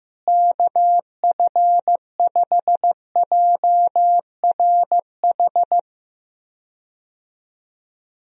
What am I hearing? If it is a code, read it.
Morse code, heard as a keyed tone, about 15 wpm, 700 Hz. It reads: KF5JRH
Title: USA Call Signs - 6 Characters - Repeated 2x - 15wpm